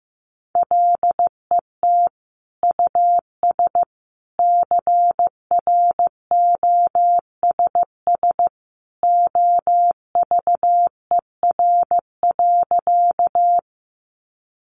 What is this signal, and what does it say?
Morse code, heard as a keyed tone, about 15 wpm, 700 Hz. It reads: LET US CROSS OVER.
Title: Sentences from Top 300 Words 15wpm